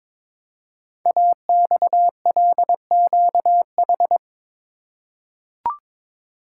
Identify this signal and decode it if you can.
Morse code, heard as a keyed tone, about 22 wpm, 700 Hz. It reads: AXLQH
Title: Code Groups - 5 Letters - 22wpm